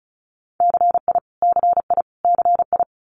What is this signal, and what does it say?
Morse code, heard as a keyed tone, about 36 wpm, 700 Hz. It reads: CI CI CI
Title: Multi-letter Phonemes Repeated 3x 35wpm